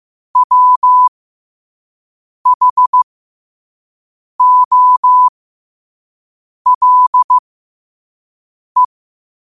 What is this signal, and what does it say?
Morse code, heard as a keyed tone, about 15 wpm, spaced as for 5 wpm, 1000 Hz. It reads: WHOLE
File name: qotdfiles / 5wpm